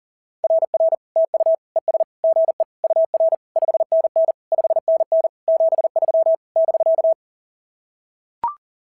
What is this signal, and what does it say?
Morse code, heard as a keyed tone, about 37 wpm, 660 Hz. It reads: RR TU ES GE UR 5NN 5NN 73 <BK>
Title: SOTA QSO Practice 40wpm